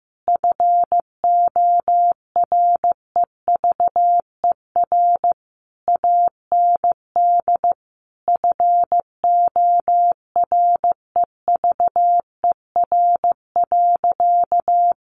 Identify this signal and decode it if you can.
Morse code, heard as a keyed tone, about 15 wpm, 700 Hz. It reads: FOREVER AND FOREVER.
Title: A Sound Of Thunder 15wpm